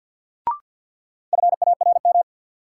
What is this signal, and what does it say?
Morse code, heard as a keyed tone, about 52 wpm, 700 Hz. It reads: FARM
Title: Sentences from Top 1000 Words - Review Part 2 - New Word Drill - 50wpm